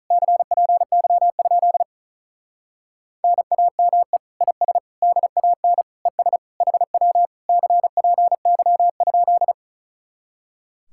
Morse code, heard as a keyed tone, about 43 wpm, 700 Hz. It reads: CPY? NAME IS DAN ES HW CPY?
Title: Speed-Racing Common QSO Phrases - 35wpm